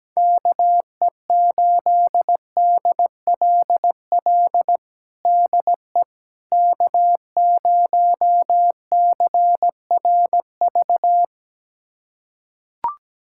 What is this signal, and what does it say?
Morse code, heard as a keyed tone, about 17 wpm, 700 Hz. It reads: KE8DLL DE K0CRV
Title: USA Callsign DE Callsign 17wpm